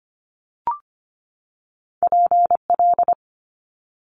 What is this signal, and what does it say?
Morse code, heard as a keyed tone, about 25 wpm, 700 Hz. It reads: PL
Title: Multi-letter Phonemes 25wpm